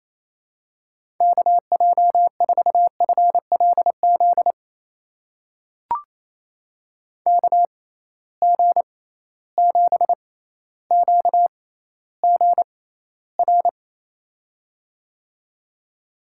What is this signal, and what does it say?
Morse code, heard as a keyed tone, about 27 wpm, spaced as for 14 wpm, 700 Hz. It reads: KJ4FLZ KG7QGR
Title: USA Call Signs - 6 Characters - 6x Character Spacing - 28wpm